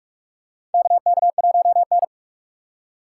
Morse code, heard as a keyed tone, about 47 wpm, 700 Hz. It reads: KK1N
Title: USA Call Signs - 4 Characters - 4x Character Spacing - 45wpm